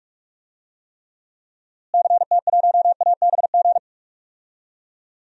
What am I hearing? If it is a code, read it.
Morse code, heard as a keyed tone, about 46 wpm, 700 Hz. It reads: CT1ABG
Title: DXCC-100 Call Signs - 6 Characters - Repeated 2x - 45wpm